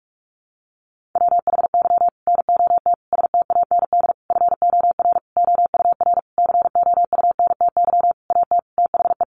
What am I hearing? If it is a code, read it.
Morse code, heard as a keyed tone, about 44 wpm, 700 Hz. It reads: WHY NOT STAND FOR OUR COUNTY AT THE
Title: Sentences from Top 1500 Words 45wpm